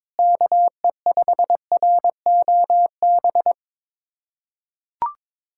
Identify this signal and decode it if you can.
Morse code, heard as a keyed tone, about 21 wpm, 700 Hz. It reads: KE5ROB
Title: USA Callsign DE Callsign 22wpm